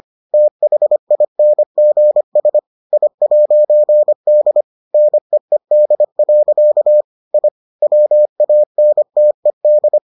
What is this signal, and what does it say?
Morse code, heard as a keyed tone, about 25 wpm, 600 Hz. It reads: THINGS I'D NEED. I WANTED